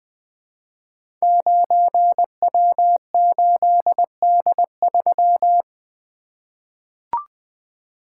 Morse code, heard as a keyed tone, about 20 wpm, 700 Hz. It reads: 9W8D3
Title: Code Groups - 5 Letter:Numbers - 20wpm